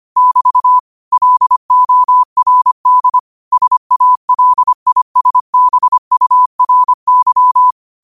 Morse code, heard as a keyed tone, about 25 wpm, 1000 Hz. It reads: <BT> LORD SALISBURY
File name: 25wpm